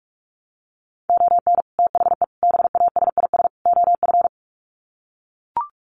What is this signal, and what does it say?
Morse code, heard as a keyed tone, about 44 wpm, 700 Hz. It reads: ON THE BASIS OF
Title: Sets of 4 Words 45wpm